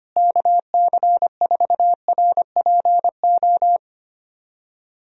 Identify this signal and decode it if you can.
Morse code, heard as a keyed tone, about 25 wpm, 700 Hz. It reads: KC4RPO
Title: USA Call Signs - 6 Characters - Repeated 3x - 25wpm